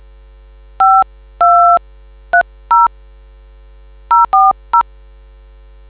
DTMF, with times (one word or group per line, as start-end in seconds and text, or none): 0.8-2.9 5230
4.1-4.8 040